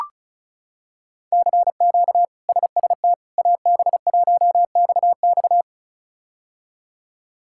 Morse code, heard as a keyed tone, about 36 wpm, 700 Hz. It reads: CQ SST AB1XX
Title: K1USN SST QSO Practice - 35wpm